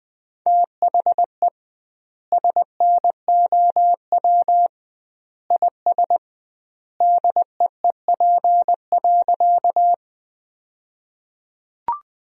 Morse code, heard as a keyed tone, about 20 wpm, 700 Hz. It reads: THE SNOW IS DEEP.
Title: 2x Word-Spacing - Sentences from Top 500 Words - 20wpm